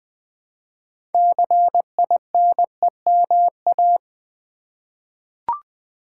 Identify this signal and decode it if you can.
Morse code, heard as a keyed tone, about 20 wpm, 700 Hz. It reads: CINEMA
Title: 6 Letter Words - First 3 Letters Given - 20wpm